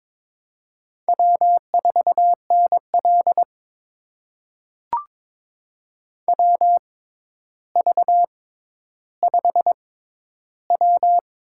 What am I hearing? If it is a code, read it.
Morse code, heard as a keyed tone, about 22 wpm, spaced as for 11 wpm, 700 Hz. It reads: W4NL WV5W
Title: USA Call Signs - 4 Characters - 6x Character Spacing - 22wpm